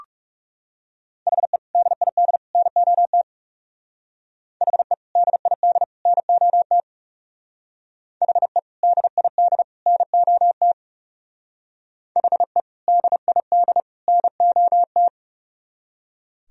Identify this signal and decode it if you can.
Morse code, heard as a keyed tone, about 37 wpm, 700 Hz. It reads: HE DID NOT HE DID NOT HE DID NOT HE DID NOT
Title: Speed-Racing - Sets of 3 Words 30wpm